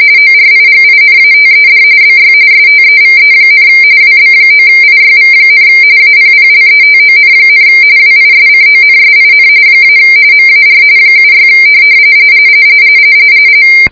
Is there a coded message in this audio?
Yes: RTTY from 0.2 to 11.5 s, about 45.45 baud. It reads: ORIGINATED BY STEVE, WB9IYV, APPLETON, WISCONSIN - MAR 76